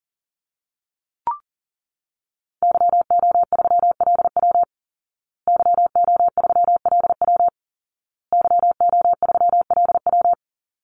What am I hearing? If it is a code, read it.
Morse code, heard as a keyed tone, about 40 wpm, 700 Hz. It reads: YO3LW YO3LW YO3LW
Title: Call Signs Repeated 2x - 40wpm